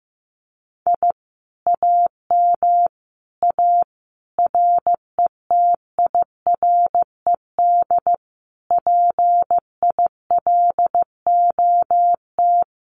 Morse code, heard as a keyed tone, about 15 wpm, 700 Hz. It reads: I AM A RETIRED PILOT
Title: Common QSO Phrases Encore 15wpm